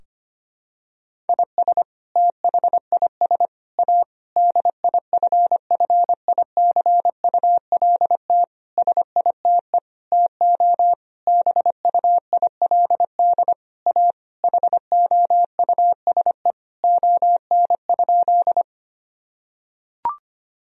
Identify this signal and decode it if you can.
Morse code, heard as a keyed tone, about 25 wpm, 700 Hz. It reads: IS THIS A DIFFICULT SITE TO BUILD A HOUSE ON?
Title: Sentences from Top 1000 Words - Review Part 4 - 25wpm